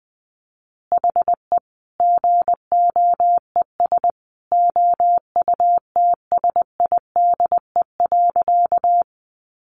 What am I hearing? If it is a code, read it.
Morse code, heard as a keyed tone, about 20 wpm, 700 Hz. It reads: HE GOES OUTSIDE.
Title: Additional Easy Phrases 20wpm